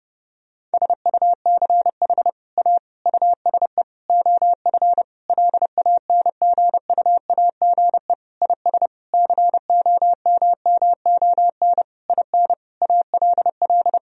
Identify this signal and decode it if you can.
Morse code, heard as a keyed tone, about 32 wpm, 700 Hz. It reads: SUCH A USE OF LANGUAGE IS COMMON IN ALL
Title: Sentences from Top 1000 Words - Review Part 3 - 30wpm